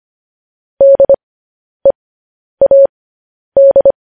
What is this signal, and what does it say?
Morse code, heard as a keyed tone, about 25 wpm, spaced as for 10 wpm, 550 Hz. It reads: DEAD
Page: news headlines in morse code at 10 wpm